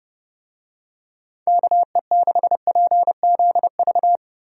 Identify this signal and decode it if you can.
Morse code, heard as a keyed tone, about 30 wpm, 700 Hz. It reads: KE6PZV
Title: USA Call Signs - 6 Characters - 30wpm